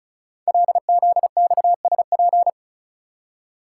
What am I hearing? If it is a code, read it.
Morse code, heard as a keyed tone, about 36 wpm, 700 Hz. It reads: LZXSP
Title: Code Groups - 5 Letters - 35wpm